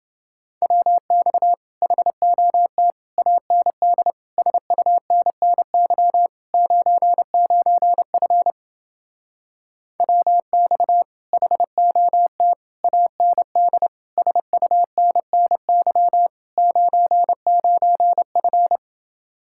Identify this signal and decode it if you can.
Morse code, heard as a keyed tone, about 28 wpm, 700 Hz. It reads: WX HOT AND SUNNY 99F WX HOT AND SUNNY 99F